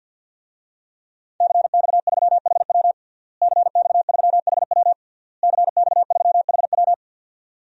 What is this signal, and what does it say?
Morse code, heard as a keyed tone, about 51 wpm, 700 Hz. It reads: CX3HW CX3HW CX3HW
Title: DXCC-100 Call Signs - 5 Characters - Repeated 3x - 50wpm